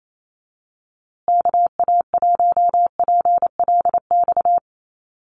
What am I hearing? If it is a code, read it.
Morse code, heard as a keyed tone, about 28 wpm, 700 Hz. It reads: KA1PLX
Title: USA Call Signs Repeated 3x 28wpm